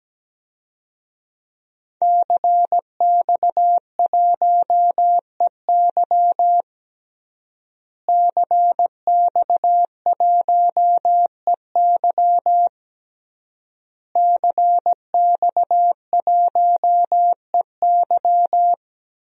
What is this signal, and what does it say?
Morse code, heard as a keyed tone, about 17 wpm, 700 Hz. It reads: CX1EY CX1EY CX1EY
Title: DXCC-100 Call Signs - 5 Characters - Repeated 3x - 17wpm